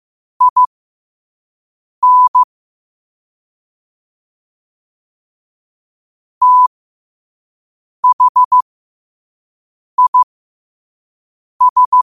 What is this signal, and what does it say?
Morse code, heard as a keyed tone, about 15 wpm, spaced as for 5 wpm, 1000 Hz. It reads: IN THIS
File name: qotdfiles / 5wpm